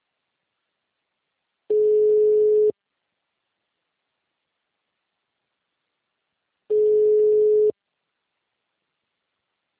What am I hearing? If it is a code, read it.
Ringback tone (Europe, 425 Hz).